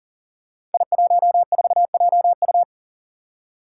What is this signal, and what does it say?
Morse code, heard as a keyed tone, about 38 wpm, 700 Hz. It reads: I14JU
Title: Five Challenging Letter-Number Combinations 40wpm